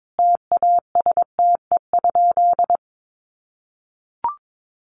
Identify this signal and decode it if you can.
Morse code, heard as a keyed tone, about 22 wpm, 700 Hz. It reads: TASTE?
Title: Sentences from Top 1100 Words 22wpm